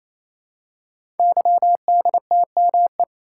Morse code, heard as a keyed tone, about 27 wpm, 700 Hz. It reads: YDTME
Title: Code Groups - 5 Letters - 28wpm